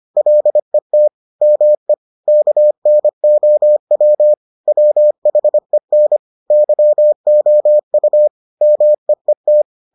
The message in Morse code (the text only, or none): LET ME KNOW WHEN YOU MEET